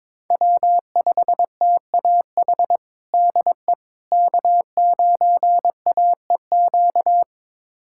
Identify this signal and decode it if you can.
Morse code, heard as a keyed tone, about 22 wpm, 700 Hz. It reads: W5TAH DE K9AEQ
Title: USA Callsign DE Callsign - 5 Characters - 22wpm